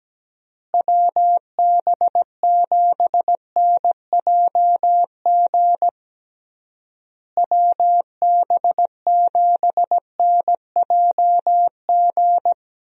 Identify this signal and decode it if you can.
Morse code, heard as a keyed tone, about 17 wpm, 700 Hz. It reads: WB7NJG WB7NJG
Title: USA Call Signs - 6 Characters - Repeated 3x - 17wpm